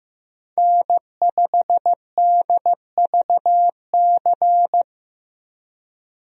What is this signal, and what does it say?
Morse code, heard as a keyed tone, about 15 wpm, 700 Hz. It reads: N5DVC
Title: USA Call Signs - 5 Characters - 15wpm